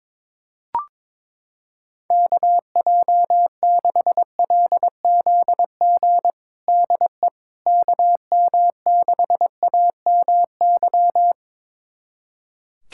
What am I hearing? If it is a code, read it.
Morse code, heard as a keyed tone, about 22 wpm, 700 Hz. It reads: KJ6LZG DE KM6AMY
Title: USA Callsign DE Callsign - 6 Characters - 22wpm